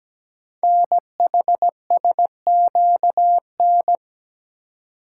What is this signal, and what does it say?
Morse code, heard as a keyed tone, about 17 wpm, 700 Hz. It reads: NHSQN
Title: Code Groups - 5 Letter:Numbers - 17wpm